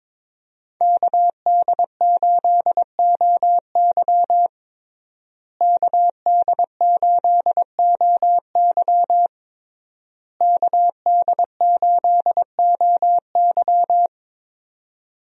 Morse code, heard as a keyed tone, about 22 wpm, 700 Hz. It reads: KD8OY KD8OY KD8OY
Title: USA Call Signs - 5 Characters - Repeated 3x - 22wpm